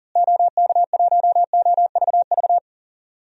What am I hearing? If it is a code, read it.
Morse code, heard as a keyed tone, about 40 wpm, 700 Hz. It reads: OK1OVV
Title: DXCC-100 Call Signs - 6 Characters - Repeated 2x - 40wpm